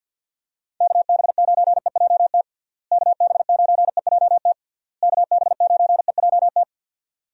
Morse code, heard as a keyed tone, about 48 wpm, 700 Hz. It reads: KB9EJT KB9EJT KB9EJT
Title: USA Call Signs - 6 Characters - Repeated 3x - 50wpm